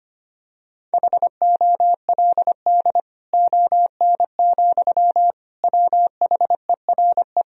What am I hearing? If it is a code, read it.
Morse code, heard as a keyed tone, about 25 wpm, 700 Hz. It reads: HOLD ON, WHERE
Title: Sentences from Top 1000 Words - Review Part 2 - 25wpm